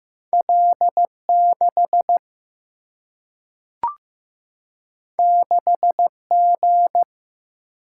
Morse code, heard as a keyed tone, about 15 wpm, 700 Hz. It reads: L 6 6 G
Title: Single-Letter Single-Number 15wpm